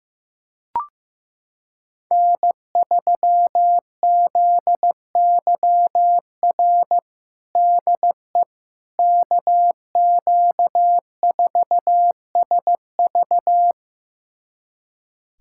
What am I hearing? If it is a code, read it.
Morse code, heard as a keyed tone, about 15 wpm, 700 Hz. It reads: N3ZYR DE KQ4SV
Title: USA Callsign DE Callsign - 5 Characters - 15wpm